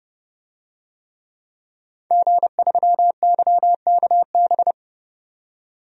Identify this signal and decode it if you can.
Morse code, heard as a keyed tone, about 32 wpm, 700 Hz. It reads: G3YKB
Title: DXCC-100 Call Signs - 5 Characters - 30wpm